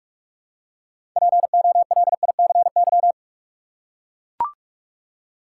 Morse code, heard as a keyed tone, about 44 wpm, 700 Hz. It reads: POLICY
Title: Six Letter Words - 45wpm